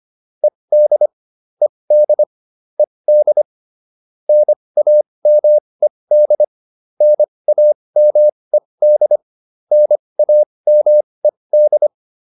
Morse code, heard as a keyed tone, about 25 wpm, spaced as for 20 wpm, 600 Hz. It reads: ED ED ED NAMED NAMED NAMED